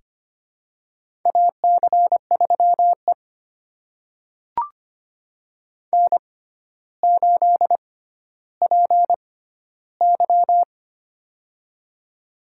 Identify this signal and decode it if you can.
Morse code, heard as a keyed tone, about 25 wpm, spaced as for 13 wpm, 700 Hz. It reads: AC3E N8PY
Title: USA Call Signs - 4 Characters - 6x Character Spacing - 25wpm